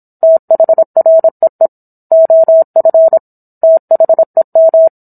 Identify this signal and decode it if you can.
Morse code, heard as a keyed tone, about 27 wpm, 650 Hz. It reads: THREE OF THEM